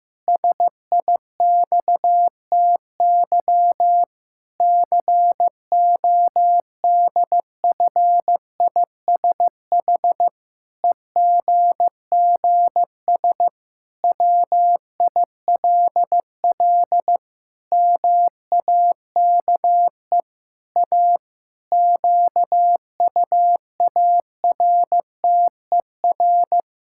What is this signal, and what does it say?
Morse code, heard as a keyed tone, about 16 wpm, 700 Hz. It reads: SIXTY CODFISH EGGS WILL MAKE A QUARTER
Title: Pangram Challenge v2 - 15wpm